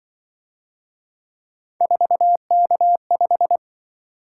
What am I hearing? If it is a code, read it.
Morse code, heard as a keyed tone, about 25 wpm, 700 Hz. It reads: 4K5